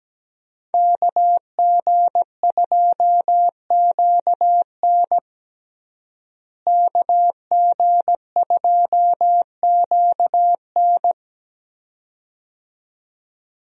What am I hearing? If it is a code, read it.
Morse code, heard as a keyed tone, about 17 wpm, 700 Hz. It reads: KG2QN KG2QN
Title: USA Call Signs - 5 Characters - Repeated 2x - 17wpm